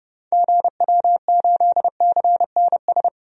Morse code, heard as a keyed tone, about 30 wpm, 700 Hz. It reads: GW8CNS
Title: DXCC-100 Call Signs - 6 Characters - Repeated 2x - 30wpm